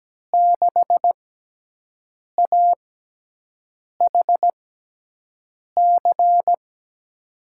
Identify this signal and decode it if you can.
Morse code, heard as a keyed tone, about 17 wpm, spaced as for 6 wpm, 700 Hz. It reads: 6AHC
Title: USA Call Signs - 6 Characters - 6x Character Spacing - 17wpm